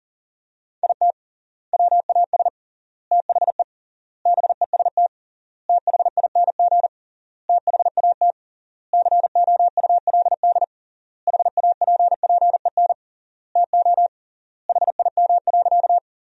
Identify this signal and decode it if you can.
Morse code, heard as a keyed tone, about 40 wpm, 700 Hz. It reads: IT WAS THE BEST THING THAT COULD HAPPEN TO HIM.
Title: 3x Word-Spacing - Sentences from Top 400 Words - 40wpm